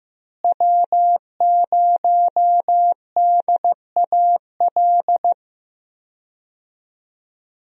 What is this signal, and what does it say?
Morse code, heard as a keyed tone, about 15 wpm, 700 Hz. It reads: W0DAL
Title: USA Call Signs - 5 Characters - Repeated 3x - 15wpm